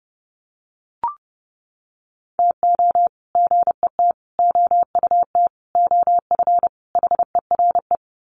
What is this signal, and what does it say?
Morse code, heard as a keyed tone, about 30 wpm, 700 Hz. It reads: TO GET OUT OF HERE
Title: Sets of 5 Words Encore 30wpm